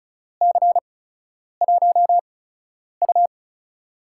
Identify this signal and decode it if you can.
Morse code, heard as a keyed tone, about 36 wpm, spaced as for 10 wpm, 700 Hz. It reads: C1U
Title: USA Call Signs - 4 Characters - 8x Character Spacing - 35wpm